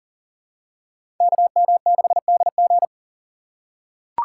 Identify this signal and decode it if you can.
Morse code, heard as a keyed tone, about 40 wpm, 700 Hz. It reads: KM6DG
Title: USA Call Signs - 5 Characters - 4x Character Spacing - 40wpm